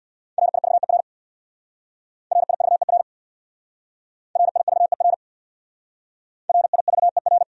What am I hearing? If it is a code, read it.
Morse code, heard as a keyed tone, about 64 wpm, 700 Hz. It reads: RIVER RIVER RIVER RIVER